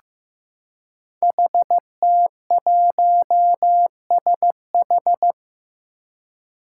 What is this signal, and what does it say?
Morse code, heard as a keyed tone, about 15 wpm, 700 Hz. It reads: HT1SH
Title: Five Challenging Letter-Number Combinations 15wpm